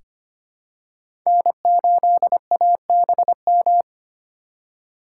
Morse code, heard as a keyed tone, about 25 wpm, 700 Hz. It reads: N8ABM
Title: USA Call Signs - 5 Characters - 3x Character Spacing - 25wpm